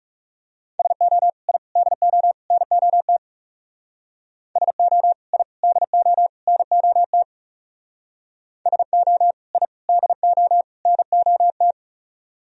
Morse code, heard as a keyed tone, about 38 wpm, 700 Hz. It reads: SO I DO NOT SO I DO NOT SO I DO NOT